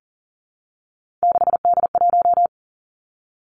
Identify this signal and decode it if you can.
Morse code, heard as a keyed tone, about 40 wpm, 700 Hz. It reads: 6D1
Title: Three Challenging Letter-Number Combinations 40wpm